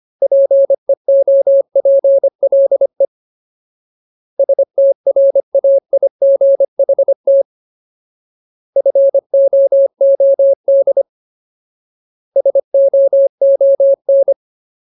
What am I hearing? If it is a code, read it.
Morse code, heard as a keyed tone, about 25 wpm, 550 Hz. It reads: PEOPLE STRAIGHT FOOD SOON
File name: Ultimate Word List / 25WPM Ultimate